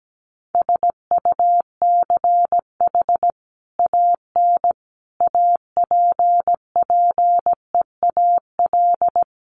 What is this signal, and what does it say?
Morse code, heard as a keyed tone, about 17 wpm, 700 Hz. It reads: SUCH AN APPEAL
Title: Sentences from Top 1300 Words 17wpm